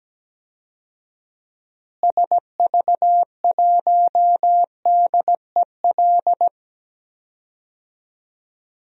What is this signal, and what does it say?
Morse code, heard as a keyed tone, about 17 wpm, 700 Hz. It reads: SV1DEL
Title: DXCC-100 Call Signs - 6 Characters - Repeated 2x - 17wpm